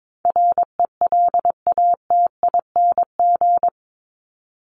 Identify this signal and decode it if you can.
Morse code, heard as a keyed tone, about 22 wpm, 700 Hz. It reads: RELATING
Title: ING Suffix - Top 100 Words with Base Words 22wpm